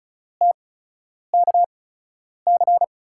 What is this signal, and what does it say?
Morse code, heard as a keyed tone, about 36 wpm, spaced as for 10 wpm, 700 Hz. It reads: TKC
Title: USA Call Signs - 5 Characters - 8x Character Spacing - 35wpm